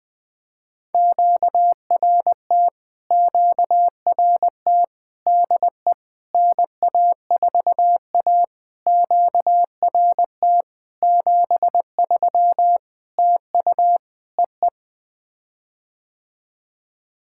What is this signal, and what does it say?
Morse code, heard as a keyed tone, about 20 wpm, 700 Hz. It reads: QRT QRT DE NA4A QRT 73 TU EE
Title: POTA QSO Practice - 20wpm